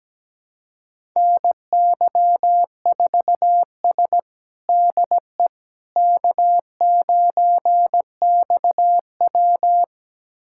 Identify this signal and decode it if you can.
Morse code, heard as a keyed tone, about 17 wpm, 700 Hz. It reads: NY4S DE K9XW
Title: USA Callsign DE Callsign - 4 Characters - 17wpm